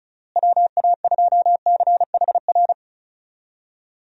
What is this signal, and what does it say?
Morse code, heard as a keyed tone, about 34 wpm, 700 Hz. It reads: WA2CHR
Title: USA Call Signs - 6 Characters - Repeated 2x - 35wpm